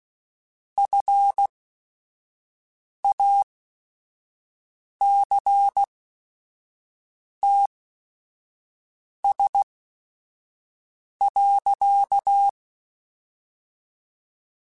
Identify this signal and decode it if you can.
Morse code, heard as a keyed tone, about 16 wpm, spaced as for 5 wpm, 800 Hz. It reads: FACTS.
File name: Quotes / Quotes-mp3s / IgnoringFacts @5